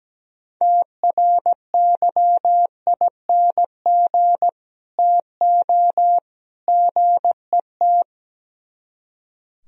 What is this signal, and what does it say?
Morse code, heard as a keyed tone, about 17 wpm, 700 Hz. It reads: TRYING TO GET
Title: Speed-Racing - Sets of 3 Words 17wpm